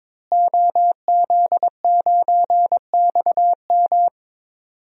Morse code, heard as a keyed tone, about 22 wpm, 700 Hz. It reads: OZ9XM
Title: DXCC-100 Call Signs - 5 Characters - Repeated 2x - 22wpm